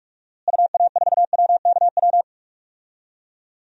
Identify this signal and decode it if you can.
Morse code, heard as a keyed tone, about 46 wpm, 700 Hz. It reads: UA4WKW
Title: DXCC-100 Call Signs - 6 Characters - Repeated 3x - 45wpm